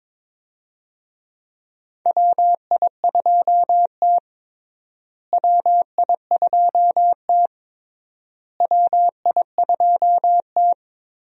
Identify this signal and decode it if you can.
Morse code, heard as a keyed tone, about 22 wpm, 700 Hz. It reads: WI2T WI2T WI2T
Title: USA Call Signs - 4 Characters - Repeated 3x - 22wpm